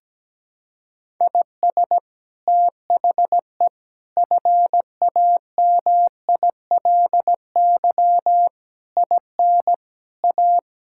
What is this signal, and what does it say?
Morse code, heard as a keyed tone, about 17 wpm, 700 Hz. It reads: IS THE FAMILY IN A
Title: Sentences from Top 1000 Words - Review Part 5 - 17wpm